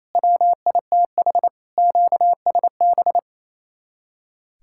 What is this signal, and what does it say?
Morse code, heard as a keyed tone, about 34 wpm, 700 Hz. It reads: WITH QSB
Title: Speed-Racing Common QSO Phrases - 28wpm